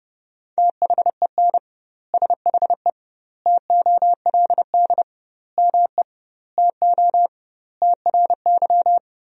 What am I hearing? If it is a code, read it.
Morse code, heard as a keyed tone, about 32 wpm, 700 Hz. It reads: THEN SHE TOLD ME TO TRY
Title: 2x Word-Spacing - Sentences from Top 400 Words - 30wpm